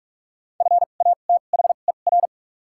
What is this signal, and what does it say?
Morse code, heard as a keyed tone, about 47 wpm, 700 Hz. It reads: F A T H E R
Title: 2x Character Spacing - Top 200 Words - 45wpm